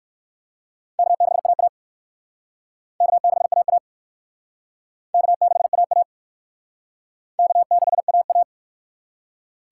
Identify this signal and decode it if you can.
Morse code, heard as a keyed tone, about 68 wpm, 700 Hz. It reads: K6AA K6AA K6AA K6AA